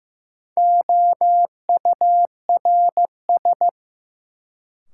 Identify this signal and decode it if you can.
Morse code, heard as a keyed tone, about 15 wpm, 700 Hz. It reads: OURS
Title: Sentences from Top 1000 Words - Review Part 5 - New Word Drill - 15wpm